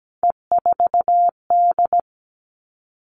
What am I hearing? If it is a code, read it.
Morse code, heard as a keyed tone, about 17 wpm, 700 Hz. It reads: E4D
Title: All Cut Numbers - 3-Digit 17wpm